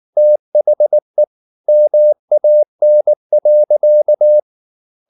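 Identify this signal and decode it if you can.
Morse code, heard as a keyed tone, about 19 wpm, 600 Hz. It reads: THE MAN.
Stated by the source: Morse code at 19 wpm from CW med Gnister, reading Ken Follett's novel Never in English.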